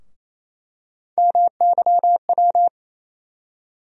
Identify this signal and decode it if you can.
Morse code, heard as a keyed tone, about 28 wpm, 700 Hz. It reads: MYW
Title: USA Call Sign Postfixes - 3 Letters 28wpm